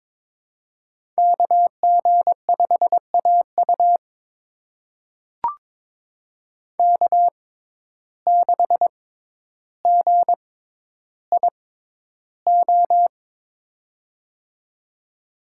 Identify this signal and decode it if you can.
Morse code, heard as a keyed tone, about 22 wpm, spaced as for 11 wpm, 700 Hz. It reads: KG5AU K6GIO
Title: USA Call Signs - 6x Character Spacing - 22wpm